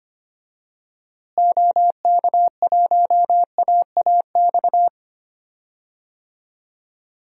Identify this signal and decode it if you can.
Morse code, heard as a keyed tone, about 25 wpm, 700 Hz. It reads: OK1AAX
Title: DXCC-100 Call Signs - 6 Characters - Repeated 2x - 25wpm